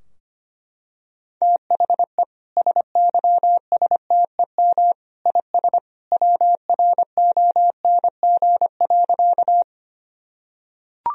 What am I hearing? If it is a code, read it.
Morse code, heard as a keyed tone, about 25 wpm, 700 Hz. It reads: THE SYSTEM IS WRONG.
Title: Sentences from Top 1000 Words - Review Part 3 - 25wpm